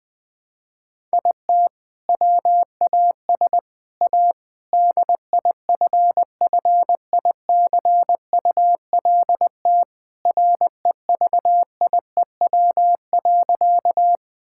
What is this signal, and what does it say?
Morse code, heard as a keyed tone, about 20 wpm, 700 Hz. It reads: IT WAS A DIFFICULT REVIEW.
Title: Sentences from Top 1000 Words - Review Part 6 - 20wpm